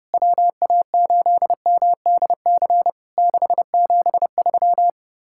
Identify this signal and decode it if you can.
Morse code, heard as a keyed tone, about 30 wpm, 700 Hz. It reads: WA8MDC 673
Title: Worked All Europe DX Contest - QTC Exchange - 30wpm